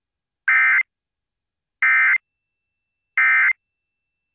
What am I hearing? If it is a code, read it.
Emergency Alert System (SAME) tones, heard as harsh buzzing bursts. It is heard from 0.5 to 3.5 s.